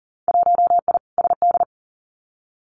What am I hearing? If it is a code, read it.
Morse code, heard as a keyed tone, about 39 wpm, 700 Hz. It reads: 1I SD
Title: Winter Field Day Exchanges 40wpm